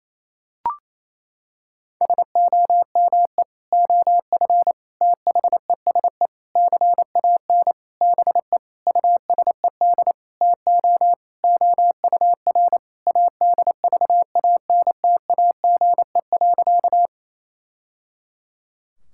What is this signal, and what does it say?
Morse code, heard as a keyed tone, about 28 wpm, 700 Hz. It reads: SOME OF THESE CAN BE USED TO OUR ADVANTAGE.
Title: Sentences from Top 1000 Words - Review Part 6 - 28wpm